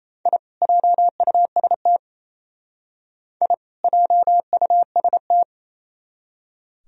Morse code, heard as a keyed tone, about 37 wpm, 700 Hz. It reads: I JUST I JUST